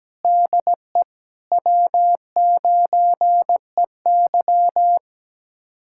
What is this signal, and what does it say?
Morse code, heard as a keyed tone, about 17 wpm, 700 Hz. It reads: DE W9EY
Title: USA Callsign DE Callsign - 4 Characters - 17wpm